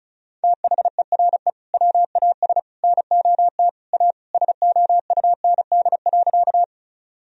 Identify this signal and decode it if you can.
Morse code, heard as a keyed tone, about 36 wpm, 700 Hz. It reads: THERE WAS NOT A SOUND.
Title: Sentences from Top 1000 Words - Review Part 1 - 35wpm